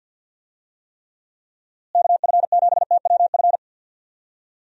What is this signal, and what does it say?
Morse code, heard as a keyed tone, about 50 wpm, 700 Hz. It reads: KF7TWF
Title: USA Call Signs - 6 Characters - Repeated 3x - 50wpm